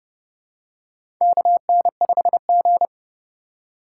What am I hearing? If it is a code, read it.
Morse code, heard as a keyed tone, about 30 wpm, 700 Hz. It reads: KN5G
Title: USA Call Signs - 4 Characters - 3x Character Spacing - 30wpm